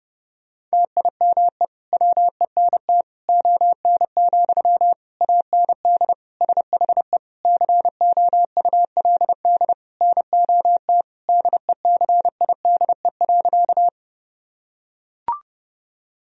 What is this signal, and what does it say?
Morse code, heard as a keyed tone, about 30 wpm, 700 Hz. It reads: TIME WENT ON, AND SHE COULD NOT DECIDE.
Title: Sentences from Top 1000 Words - Review Part 3 - 30wpm